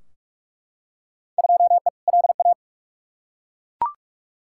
Morse code, heard as a keyed tone, about 42 wpm, 700 Hz. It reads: 2E LA
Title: Field Day Exchanges Encore - 45wpm